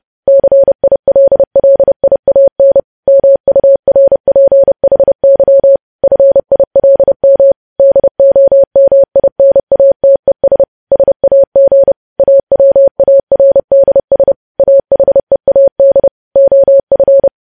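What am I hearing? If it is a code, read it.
Morse code, heard as a keyed tone, about 30 wpm, 550 Hz. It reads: CILLIAN MURPHY FILM DOMINATES SAG AWARDS AHEAD OF
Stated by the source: News headlines sent at 30 wpm.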